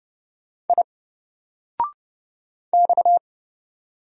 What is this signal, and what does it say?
Morse code, heard as a keyed tone, about 30 wpm, 700 Hz. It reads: I X
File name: Single-Letter-Number-30wpm